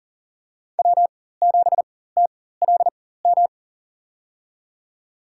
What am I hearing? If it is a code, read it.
Morse code, heard as a keyed tone, about 40 wpm, spaced as for 19 wpm, 700 Hz. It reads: W7TLM
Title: USA Call Signs - 5 Characters - 4x Character Spacing - 40wpm